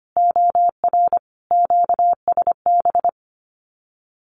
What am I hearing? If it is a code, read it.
Morse code, heard as a keyed tone, about 25 wpm, 700 Hz. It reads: OR QSB
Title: Common QSO Phrases 25wpm